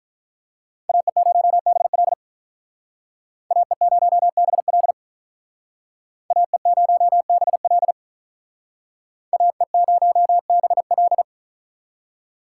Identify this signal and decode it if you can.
Morse code, heard as a keyed tone, about 45 wpm, 700 Hz. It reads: AE0BL AE0BL AE0BL AE0BL